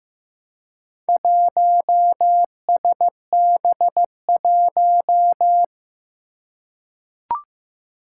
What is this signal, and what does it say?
Morse code, heard as a keyed tone, about 15 wpm, 700 Hz. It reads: 1SB1
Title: Four Challenging Letter-Number Combinations 15wpm